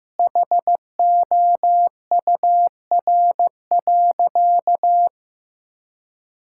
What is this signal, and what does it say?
Morse code, heard as a keyed tone, about 15 wpm, 700 Hz. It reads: HOUR.
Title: Sentences from Top 1000 Words - Review Part 5 - 15wpm